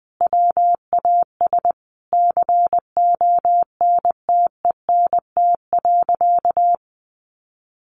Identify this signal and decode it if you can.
Morse code, heard as a keyed tone, about 20 wpm, 700 Hz. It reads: WAS CONTENT.
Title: Sentences from Top 1400 Words 20wpm